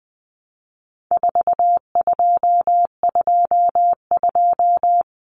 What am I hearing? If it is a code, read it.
Morse code, heard as a keyed tone, about 20 wpm, 700 Hz. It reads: 4222
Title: Common Cut Numbers - 4-Digit 20wpm